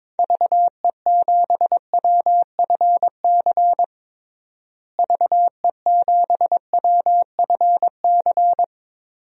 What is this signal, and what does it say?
Morse code, heard as a keyed tone, about 22 wpm, 700 Hz. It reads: VE7WFC VE7WFC
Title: DXCC-100 Call Signs - 6 Characters - Repeated 3x - 22wpm